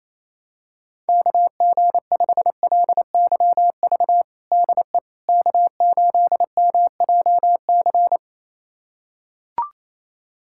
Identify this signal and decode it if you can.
Morse code, heard as a keyed tone, about 28 wpm, 700 Hz. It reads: KG5LYV DE K8MJC
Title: USA Callsign DE Callsign 28wpm